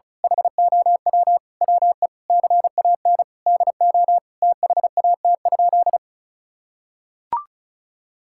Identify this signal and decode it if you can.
Morse code, heard as a keyed tone, about 36 wpm, 700 Hz. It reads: HOW WE CAN DO THAT?
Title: Sentences from Top 1000 Words - Review Part 6 - 35wpm